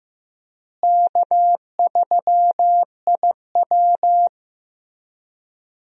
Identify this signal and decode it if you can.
Morse code, heard as a keyed tone, about 15 wpm, 700 Hz. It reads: K3IW
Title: USA Call Signs - 4 Characters - Repeated 3x - 15wpm